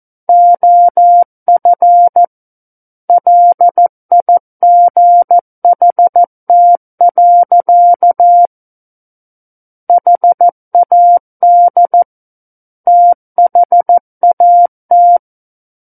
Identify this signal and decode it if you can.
Morse code, heard as a keyed tone, about 14 wpm, 700 Hz. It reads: OF LIGHT. HAD THAT
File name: Book One / 05-Chapter 5 - 14 WPM